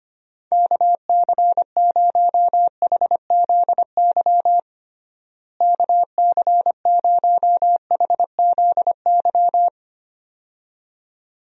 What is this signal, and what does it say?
Morse code, heard as a keyed tone, about 25 wpm, 700 Hz. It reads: KC0HZY KC0HZY
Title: USA Call Signs - 6 Characters - Repeated 2x - 25wpm